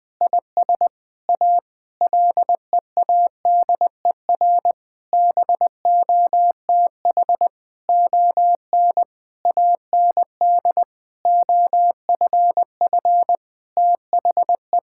Morse code, heard as a keyed tone, about 19 wpm, 700 Hz. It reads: IS A LEADER BOTH ON AND OFF THE
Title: Sentences with 6-Word Phrases - 20wpm